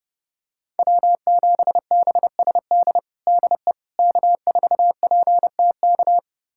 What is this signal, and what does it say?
Morse code, heard as a keyed tone, about 32 wpm, 700 Hz. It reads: W7BSD DE K4PTK
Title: USA Callsign DE Callsign - 5 Characters - 30wpm